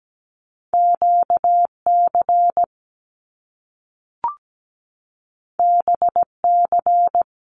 Morse code, heard as a keyed tone, about 17 wpm, 700 Hz. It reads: QC BC
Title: Canadian Provinces and Territories 17wpm